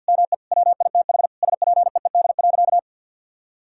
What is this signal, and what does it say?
Morse code, heard as a keyed tone, about 50 wpm, 700 Hz. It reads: ME WITH SPEED.
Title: Sentences from Top 1000 Words - Review Part 5 - 50wpm